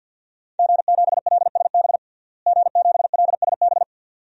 Morse code, heard as a keyed tone, about 48 wpm, 700 Hz. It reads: G7LSB G7LSB
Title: DXCC-100 Call Signs - 5 Characters - Repeated 2x - 50wpm